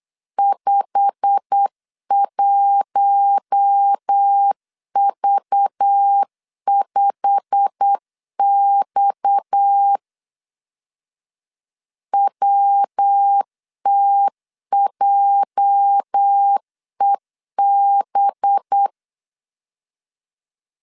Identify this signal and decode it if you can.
Morse code, heard as a keyed tone, about 8 wpm, 800 Hz. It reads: 51V5X WTJEB
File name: LEZIONI / 8 YJ45 / 8 YJ45 8wpm 3214